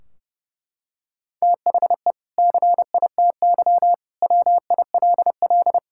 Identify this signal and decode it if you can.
Morse code, heard as a keyed tone, about 30 wpm, 700 Hz. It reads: THE CITY WILL
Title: Sentences from Top 1600 Words 30wpm